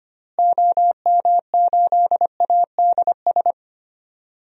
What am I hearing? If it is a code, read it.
Morse code, heard as a keyed tone, about 25 wpm, 700 Hz. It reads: OM8ADS
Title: DXCC-100 Call Signs - 6 Characters - Repeated 3x - 25wpm